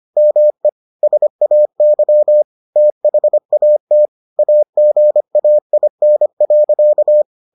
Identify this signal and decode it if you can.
Morse code, heard as a keyed tone, about 25 wpm, 600 Hz. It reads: ME SAY THAT AGAIN.